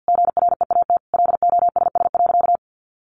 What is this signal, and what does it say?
Morse code, heard as a keyed tone, about 50 wpm, 700 Hz. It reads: GREAT LOSS.
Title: Sentences from Top 1400 Words 50wpm